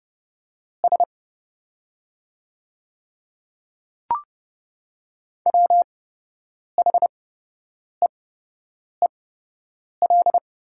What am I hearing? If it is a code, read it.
Morse code, heard as a keyed tone, about 29 wpm, spaced as for 8 wpm, 700 Hz. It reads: S WHEEL
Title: 8x Character Spacing - Top 500 Words - 30wpm